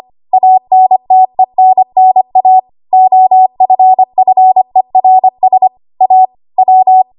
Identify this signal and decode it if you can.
Morse code, heard as a keyed tone, about 25 wpm, 750 Hz. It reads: ANTENNA OFFERS A W
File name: Archive / 25 WPM / 25WPM